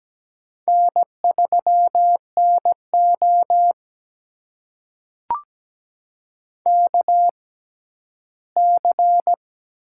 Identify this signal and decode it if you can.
Morse code, heard as a keyed tone, about 17 wpm, spaced as for 12 wpm, 700 Hz. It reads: N3NO KC
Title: USA Call Signs - 4 Characters - 6x Character Spacing - 17wpm